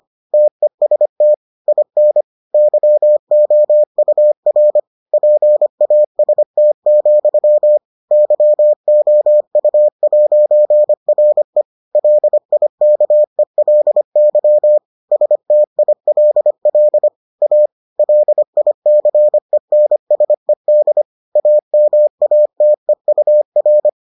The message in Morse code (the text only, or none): TEST IN YOUR PAST, YOU'RE LIKELY STILL A LICENSED AMATEUR